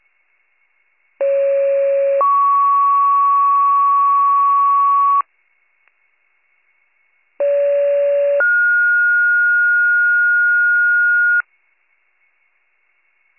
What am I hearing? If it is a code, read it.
Two-tone page, heard as a one-second tone, then a three-second tone, about 550 then 1120 Hz, about 570 then 1430 Hz. It is heard from 1.2 to 11.4 s.